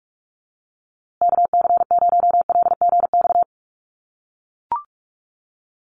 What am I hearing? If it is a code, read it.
Morse code, heard as a keyed tone, about 44 wpm, 700 Hz. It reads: KC0LGX
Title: USA Call Signs Repeated 2x 45wpm